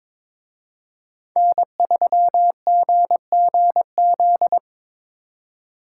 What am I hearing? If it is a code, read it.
Morse code, heard as a keyed tone, about 26 wpm, 700 Hz. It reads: N3GGZ
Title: Speed Racing - USA 1x3 Call Signs 22wpm